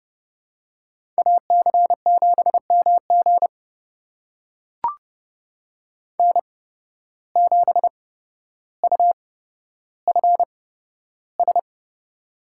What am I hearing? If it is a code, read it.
Morse code, heard as a keyed tone, about 30 wpm, spaced as for 13 wpm, 700 Hz. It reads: AC7MG N7UFS
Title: USA Call Signs - 5 Characters - 8x Character Spacing - 30wpm